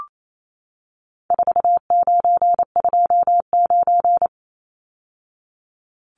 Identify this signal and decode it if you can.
Morse code, heard as a keyed tone, about 28 wpm, 700 Hz. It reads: 4929
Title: Numbers - 4-Digit 28wpm